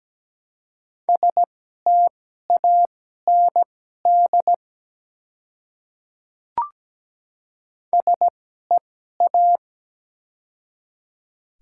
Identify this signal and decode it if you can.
Morse code, heard as a keyed tone, about 17 wpm, spaced as for 12 wpm, 700 Hz. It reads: STAND SEA
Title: 2x Character Spacing - Top 300 Words - 17wpm